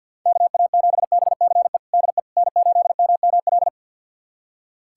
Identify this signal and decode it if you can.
Morse code, heard as a keyed tone, about 51 wpm, 700 Hz. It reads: KA7BCE DE N8MML
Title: USA Callsign DE Callsign 50wpm